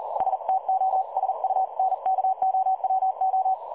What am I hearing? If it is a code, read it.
Morse code, heard as a keyed tone, about 40 wpm, 760 Hz. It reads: STOSUNKOWO
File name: dlazaawansowanych / nagraniazpasma / websdr recording 3565.0kHz